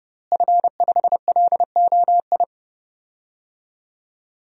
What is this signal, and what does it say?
Morse code, heard as a keyed tone, about 32 wpm, 700 Hz. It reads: F5LOI
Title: DXCC-100 Call Signs - 5 Characters - Repeated 3x - 30wpm